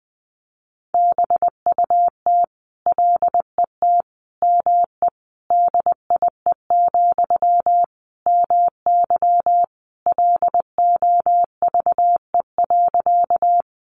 Morse code, heard as a keyed tone, about 20 wpm, 700 Hz. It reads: BUT LET ME DIE, MY LOVE.
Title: Sentences from Top 1300 Words 20wpm